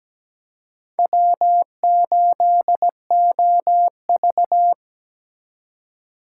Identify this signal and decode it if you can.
Morse code, heard as a keyed tone, about 20 wpm, 700 Hz. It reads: W8OV
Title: Speed Racing - USA 1x2 Call Signs 17wpm